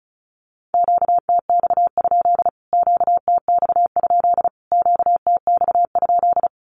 Morse code, heard as a keyed tone, about 34 wpm, 700 Hz. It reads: QTX? QTX? QTX?
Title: Q-Codes Repeated 3x 35wpm